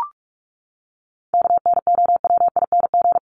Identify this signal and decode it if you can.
Morse code, heard as a keyed tone, about 44 wpm, 700 Hz. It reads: KNOWING
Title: ING Suffix - Top 100 Words with Base Words 45wpm